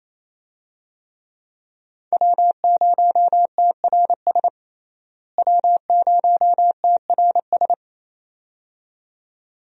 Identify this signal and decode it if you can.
Morse code, heard as a keyed tone, about 28 wpm, 700 Hz. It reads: W0TRS W0TRS
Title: USA Call Signs - 5 Characters - Repeated 2x - 28wpm